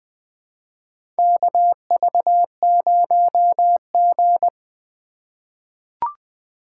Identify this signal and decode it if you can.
Morse code, heard as a keyed tone, about 20 wpm, 700 Hz. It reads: KV0G
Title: USA Call Signs - 4 Characters - 2x Character Spacing - 20wpm